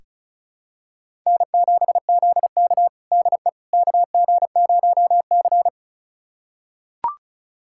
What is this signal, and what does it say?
Morse code, heard as a keyed tone, about 36 wpm, 700 Hz. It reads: N7ZK DE KG0C
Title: USA Callsign DE Callsign - 4 Characters - 35wpm